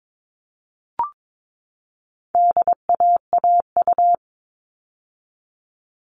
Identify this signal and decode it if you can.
Morse code, heard as a keyed tone, about 22 wpm, 700 Hz. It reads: DAAU
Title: All Cut Numbers - 4-Digit 22wpm